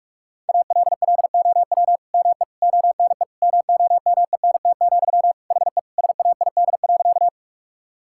Morse code, heard as a keyed tone, about 46 wpm, 700 Hz. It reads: ALLOW ME ONE MOMENT, HE SAID.
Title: Sentences from Top 1000 Words - Review Part 5 - 45wpm